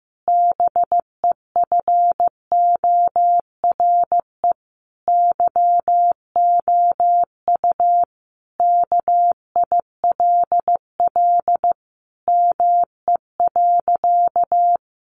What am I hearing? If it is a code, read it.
Morse code, heard as a keyed tone, about 15 wpm, 700 Hz. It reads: BEFORE YOU KILL ME.
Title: Sentences from Top 1400 Words 15wpm